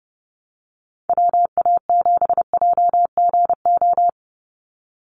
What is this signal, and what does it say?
Morse code, heard as a keyed tone, about 29 wpm, 700 Hz. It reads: WA7JGO
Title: USA Call Signs 30wpm